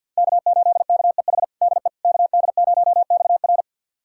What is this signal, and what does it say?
Morse code, heard as a keyed tone, about 52 wpm, 700 Hz. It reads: K8KEH DE KD0XR
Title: USA Callsign DE Callsign - 5 Characters - 50wpm